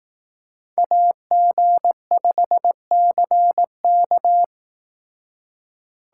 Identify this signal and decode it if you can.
Morse code, heard as a keyed tone, about 18 wpm, 700 Hz. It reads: AG5CK
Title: Speed Racing - USA 2x2 Call Signs 15wpm